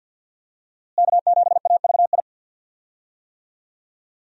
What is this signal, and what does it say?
Morse code, heard as a keyed tone, about 50 wpm, 700 Hz. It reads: K7AVI
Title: USA Call Signs - 5 Characters - Repeated 2x - 50wpm